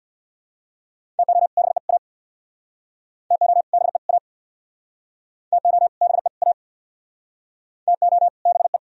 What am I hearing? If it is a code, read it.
Morse code, heard as a keyed tone, about 64 wpm, 700 Hz. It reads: TO BE A TO BE A TO BE A TO BE